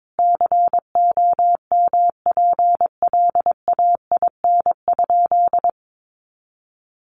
Morse code, heard as a keyed tone, about 22 wpm, 700 Hz. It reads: COMPLAIN?
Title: Sentences from Top 1300 Words 22wpm